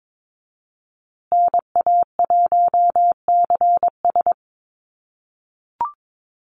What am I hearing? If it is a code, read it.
Morse code, heard as a keyed tone, about 22 wpm, 700 Hz. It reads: NA1CS
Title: USA Call Signs Repeated 2x 22wpm